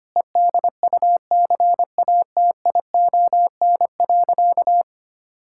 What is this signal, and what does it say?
Morse code, heard as a keyed tone, about 25 wpm, 700 Hz. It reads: EDUCATION.
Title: Sentences from Top 1000 Words - Review Part 6 - 25wpm